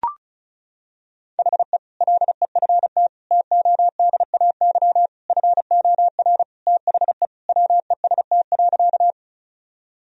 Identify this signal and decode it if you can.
Morse code, heard as a keyed tone, about 34 wpm, 700 Hz. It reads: HE LEFT TODAY FOR THE WEST.
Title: Sentences from Top 1000 Words - Review Part 2 - 35wpm